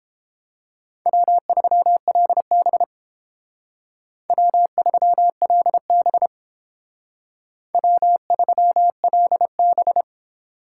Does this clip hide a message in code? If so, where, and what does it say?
Yes: Morse code from 1.1 to 10.0 s, heard as a keyed tone, about 30 wpm, 700 Hz. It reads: W3LB W3LB W3LB